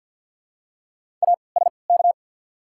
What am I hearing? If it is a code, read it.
Morse code, heard as a keyed tone, about 50 wpm, spaced as for 28 wpm, 700 Hz. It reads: ASK
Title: 3x Character Spacing - Top 200 Words - 50wpm